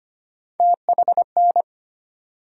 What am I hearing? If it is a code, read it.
Morse code, heard as a keyed tone, about 25 wpm, 700 Hz. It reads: THN
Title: USA Call Sign Postfixes - 3 Letters 25wpm